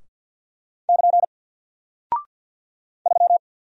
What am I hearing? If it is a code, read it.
Morse code, heard as a keyed tone, about 48 wpm, 700 Hz. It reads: <KN> 3
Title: Single Character (Letter:Number:Punctuation:Prosign) 50wpm